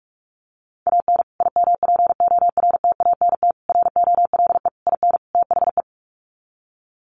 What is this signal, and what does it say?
Morse code, heard as a keyed tone, about 44 wpm, 700 Hz. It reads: AN IMPORTANT ROLE IN THE
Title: Sets of 5 Words Encore2 45wpm